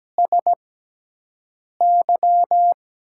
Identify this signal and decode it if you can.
Morse code, heard as a keyed tone, about 17 wpm, spaced as for 6 wpm, 700 Hz. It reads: SY
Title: USA Call Signs - 6 Characters - 6x Character Spacing - 17wpm